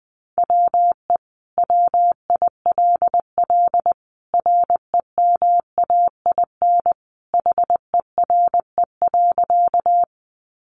Morse code, heard as a keyed tone, about 20 wpm, 700 Hz. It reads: WE WILL REMAIN HERE.
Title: Sentences from Top 1200 Words 20wpm